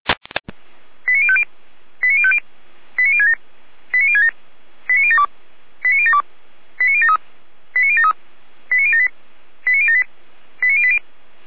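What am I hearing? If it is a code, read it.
Five-tone selective call (ZVEI), heard as quick runs of five tones, tones of about 70 ms. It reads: 89050 89050 89067 89067 89072 89072 89073 89073 89078 89078 89080
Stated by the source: ZVEI five-tone selective call received via SDR.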